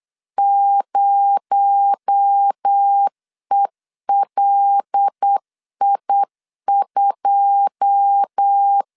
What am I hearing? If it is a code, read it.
Morse code, heard as a keyed tone, about 8 wpm, 800 Hz. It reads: 0ELI2
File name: CW / DET / 1 9 1500C S C 8wpm 5653